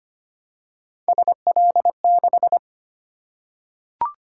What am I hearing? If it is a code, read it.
Morse code, heard as a keyed tone, about 25 wpm, 700 Hz. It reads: SL6
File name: Three Letter-Number Combinations - 25wpm